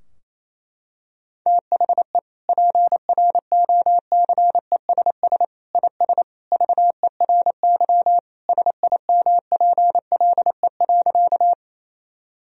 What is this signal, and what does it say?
Morse code, heard as a keyed tone, about 28 wpm, 700 Hz. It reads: THE PROCESS IS VERY SIMPLE.
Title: Sentences from Top 1000 Words - Review Part 3 - 28wpm